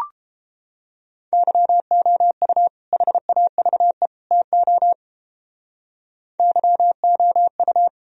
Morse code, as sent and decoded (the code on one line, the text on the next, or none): -.-- --- ..- / .... .- ...- . / - --- / -.-- --- ..-
YOU HAVE TO YOU